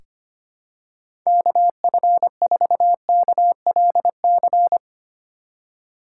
Morse code, heard as a keyed tone, about 25 wpm, 700 Hz. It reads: KF4KLC
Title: USA Call Signs - 6 Characters - 4x Character Spacing - 25wpm